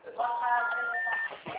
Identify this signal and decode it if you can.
Special information tone, with speech.